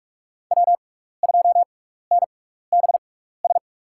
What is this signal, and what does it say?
Morse code, heard as a keyed tone, about 44 wpm, spaced as for 15 wpm, 700 Hz. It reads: W2NBS
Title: USA Call Signs - 5 Characters - 6x Character Spacing - 45wpm